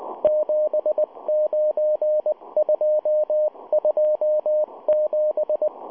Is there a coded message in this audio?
Yes: Morse code from 0.2 to 5.7 s, heard as a keyed tone, about 19 wpm, 610 Hz. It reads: 79227